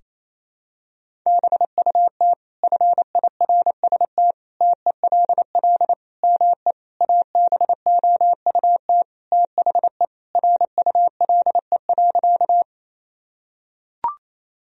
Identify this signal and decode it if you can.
Morse code, heard as a keyed tone, about 27 wpm, 700 Hz. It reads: BUT FIRST TELL ME ABOUT THE RULE.
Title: Sentences from Top 1000 Words - Review Part 2 - 28wpm